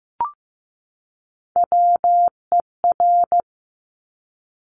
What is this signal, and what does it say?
Morse code, heard as a keyed tone, about 15 wpm, 700 Hz. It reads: WER
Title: Top Three Letter Combinations in Words 15wpm